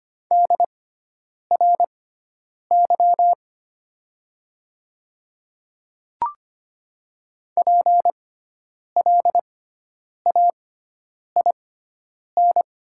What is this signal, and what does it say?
Morse code, heard as a keyed tone, about 25 wpm, spaced as for 9 wpm, 700 Hz. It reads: DRY PLAIN
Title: 6x Character Spacing - Top 500 Words - 25wpm